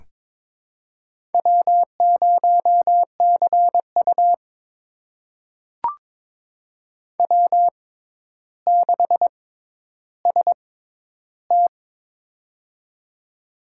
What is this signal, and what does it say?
Morse code, heard as a keyed tone, about 22 wpm, spaced as for 11 wpm, 700 Hz. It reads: W0CU W6ST
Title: USA Call Signs - 4 Characters - 6x Character Spacing - 22wpm